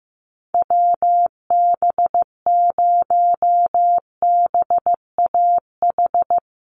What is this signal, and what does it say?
Morse code, heard as a keyed tone, about 15 wpm, 700 Hz. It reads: WB0BAH
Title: USA Call Signs Repeated 3x 15wpm